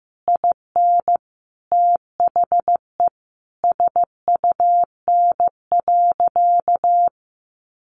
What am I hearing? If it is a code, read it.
Morse code, heard as a keyed tone, about 14 wpm, 700 Hz. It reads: IN THE SUN.
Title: Sentences from Top 300 Words 15wpm